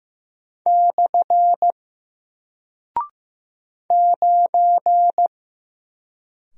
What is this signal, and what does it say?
Morse code, heard as a keyed tone, about 15 wpm, 700 Hz. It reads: / 9
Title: Single Character (Letter:Number:Punctuation:Prosign) Alt 15wpm